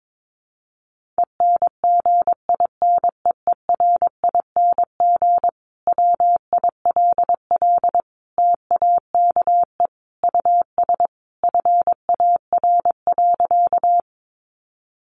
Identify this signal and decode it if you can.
Morse code, heard as a keyed tone, about 22 wpm, 700 Hz. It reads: ENGINEERING WILL TAKE US FAR.
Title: Sentences from Top 1500 Words 22wpm